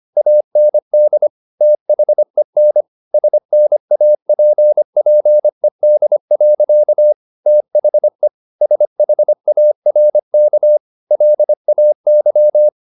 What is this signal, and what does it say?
Morse code, heard as a keyed tone, about 25 wpm, 600 Hz. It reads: AND THEN SNAPPED. THE SHARK LAY